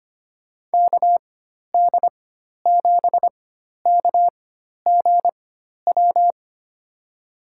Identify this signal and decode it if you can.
Morse code, heard as a keyed tone, about 25 wpm, spaced as for 12 wpm, 700 Hz. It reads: KD7KGW
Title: USA Call Signs - 4x Character Spacing - 25wpm